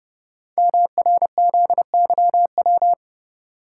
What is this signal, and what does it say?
Morse code, heard as a keyed tone, about 28 wpm, 700 Hz. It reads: MRZYW
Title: Code Groups - 5 Letters - 30wpm